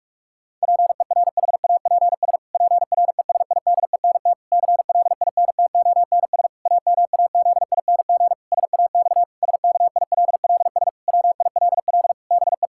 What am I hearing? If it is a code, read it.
Morse code, heard as a keyed tone, about 43 wpm, 700 Hz. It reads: PERHAPS PRESIDENT CLINTONS AMAZING SAX SKILLS WILL BE
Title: Pangram Challenge v2 - 45wpm